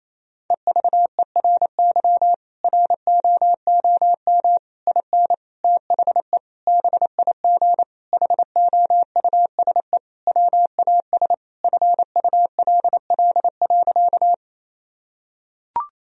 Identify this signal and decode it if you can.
Morse code, heard as a keyed tone, about 28 wpm, 700 Hz. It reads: EVERY ROOM IN THE BIG HOUSE WAS FULL.
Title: Sentences from Top 1000 Words - Review Part 3 - 28wpm